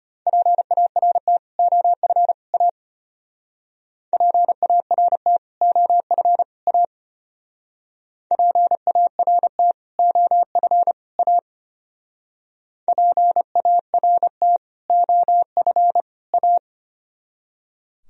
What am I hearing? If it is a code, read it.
Morse code, heard as a keyed tone, about 33 wpm, 700 Hz. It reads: PART OF A PART OF A PART OF A PART OF A